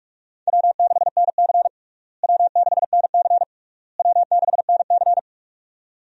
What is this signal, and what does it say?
Morse code, heard as a keyed tone, about 47 wpm, 700 Hz. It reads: W6NC W6NC W6NC
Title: USA Call Signs - 4 Characters - Repeated 3x - 45wpm